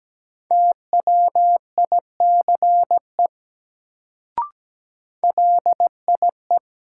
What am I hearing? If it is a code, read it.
Morse code, heard as a keyed tone, about 17 wpm, 700 Hz. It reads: TWICE LIE
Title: Sentences from Top 1600 Words - New Word Drill - 17wpm